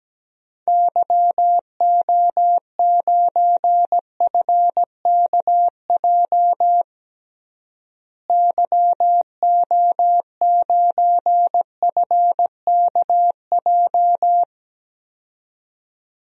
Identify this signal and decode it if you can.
Morse code, heard as a keyed tone, about 17 wpm, 700 Hz. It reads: YO9FKJ YO9FKJ
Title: DXCC-100 Call Signs - 6 Characters - Repeated 2x - 17wpm